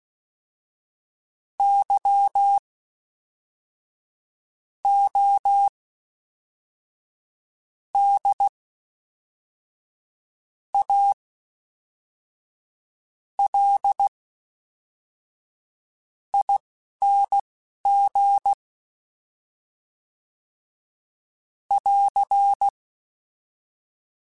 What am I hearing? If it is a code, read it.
Morse code, heard as a keyed tone, about 16 wpm, spaced as for 4 wpm, 800 Hz. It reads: YODALING <AR>
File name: Special INGs / Special INGs-mp3s / Fast ing @10